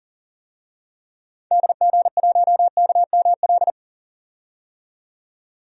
Morse code, heard as a keyed tone, about 39 wpm, 700 Hz. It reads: DG1KML
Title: DXCC-100 Call Signs - 6 Characters - 40wpm